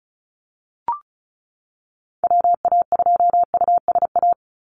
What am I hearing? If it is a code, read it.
Morse code, heard as a keyed tone, about 34 wpm, 700 Hz. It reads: WA2USA
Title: NA SOTA Chaser Callsigns 35wpm